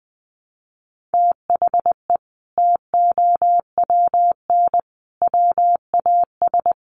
Morse code, heard as a keyed tone, about 19 wpm, 700 Hz. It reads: THE TOWN WAS
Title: Sentences from Top 1300 Words 20wpm